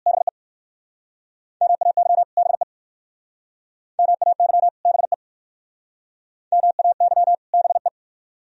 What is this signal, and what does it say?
Morse code, heard as a keyed tone, about 59 wpm, 690 Hz. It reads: BE MAY BE MAY BE MAY BE